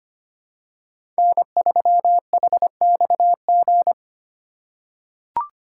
Morse code, heard as a keyed tone, about 25 wpm, 700 Hz. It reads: N3HXG
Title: USA Call Signs - 5 Characters - 2x Character Spacing - 25wpm